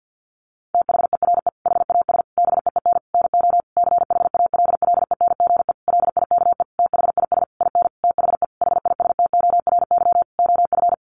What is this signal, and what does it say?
Morse code, heard as a keyed tone, about 50 wpm, 700 Hz. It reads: THERE HAS BEEN NO CHALLENGE LIKE THIS IN THE HISTORY OF
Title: Sentences from Top 1500 Words 50wpm